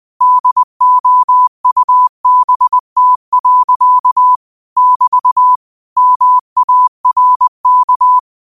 Morse code, heard as a keyed tone, about 20 wpm, 1000 Hz. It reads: DOUBT. <BT> MARK
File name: qotdfiles / 20wpm